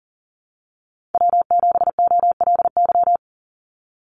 Morse code, heard as a keyed tone, about 37 wpm, 700 Hz. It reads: W7OLY
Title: Call Signs Repeated 2x - 40wpm